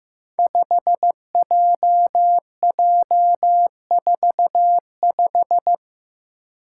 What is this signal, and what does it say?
Morse code, heard as a keyed tone, about 15 wpm, 700 Hz. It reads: 5JJ45
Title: Five Challenging Letter-Number Combinations 15wpm